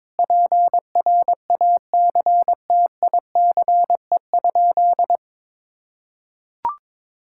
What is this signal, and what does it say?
Morse code, heard as a keyed tone, about 22 wpm, 700 Hz. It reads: PRACTICE?
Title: Sentences from Top 1000 Words - Review Part 6 - 22wpm